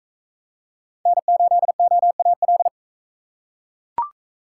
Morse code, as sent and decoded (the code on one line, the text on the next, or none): -. ---.. --- .- .-..
N8OAL